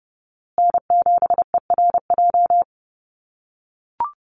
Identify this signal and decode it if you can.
Morse code, heard as a keyed tone, about 30 wpm, 700 Hz. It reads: N7ERJ
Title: USA Call Signs Repeated 2x 30wpm